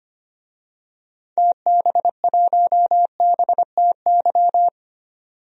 Morse code, heard as a keyed tone, about 25 wpm, 700 Hz. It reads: TB1BTY
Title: DXCC-100 Call Signs - 6 Characters - 25wpm